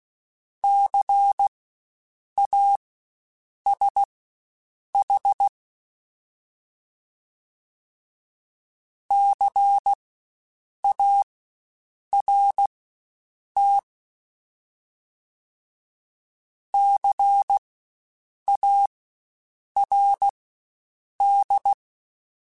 Morse code, heard as a keyed tone, about 16 wpm, spaced as for 7 wpm, 800 Hz. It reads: CASH CART CARD
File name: Spaced Alphas / Spaced Alphas mp3s / Cs sp @15